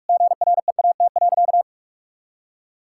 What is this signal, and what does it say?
Morse code, heard as a keyed tone, about 46 wpm, 700 Hz. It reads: GREAT.
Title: Sentences from Top 1000 Words - Review Part 5 - 45wpm